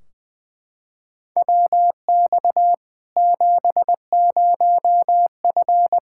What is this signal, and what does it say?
Morse code, heard as a keyed tone, about 21 wpm, 700 Hz. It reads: WX 70F
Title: K1USN SST QSO Practice 20wpm